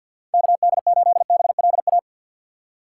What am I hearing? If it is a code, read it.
Morse code, heard as a keyed tone, about 53 wpm, 700 Hz. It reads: KD8BLA
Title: USA Call Signs - 6 Characters - 8x Character Spacing - 50wpm